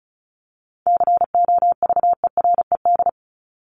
Morse code, heard as a keyed tone, about 34 wpm, 700 Hz. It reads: COVERED
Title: ED Suffix - Top 100 Words with Base Words 35wpm